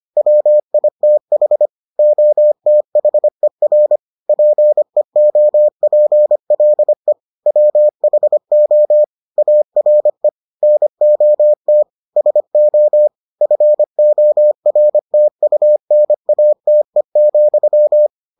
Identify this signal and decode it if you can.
Morse code, heard as a keyed tone, about 25 wpm, 600 Hz. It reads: WITH OTHER PEOPLE WHO ARE NOT SO FORTUNATE,